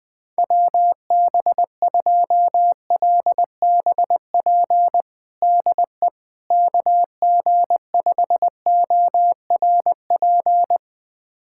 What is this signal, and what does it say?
Morse code, heard as a keyed tone, about 20 wpm, 700 Hz. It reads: WB2LBP DE KG5ORP
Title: USA Callsign DE Callsign - 6 Characters - 20wpm